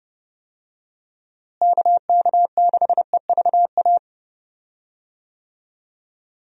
Morse code, heard as a keyed tone, about 30 wpm, 700 Hz. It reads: KK6EVA
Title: USA Call Signs - 6 Characters - Repeated 2x - 30wpm